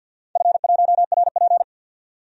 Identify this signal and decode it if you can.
Morse code, heard as a keyed tone, about 52 wpm, 700 Hz. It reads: F1RP
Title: DXCC-100 Call Signs - 4 Characters - 50wpm